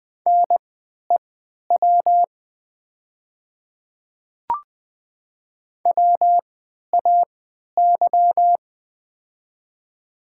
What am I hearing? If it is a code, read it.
Morse code, heard as a keyed tone, about 20 wpm, spaced as for 11 wpm, 700 Hz. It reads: NEW WAY
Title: 3x Character Spacing - Top 300 Words - 20wpm